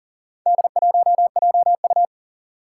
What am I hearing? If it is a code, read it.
Morse code, heard as a keyed tone, about 40 wpm, 700 Hz. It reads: D1JU
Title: Four Challenging Letter-Number Combinations 40wpm